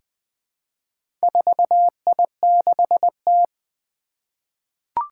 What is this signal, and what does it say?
Morse code, heard as a keyed tone, about 20 wpm, 700 Hz. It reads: 4I6T
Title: Four Challenging Letter-Number Combinations 20wpm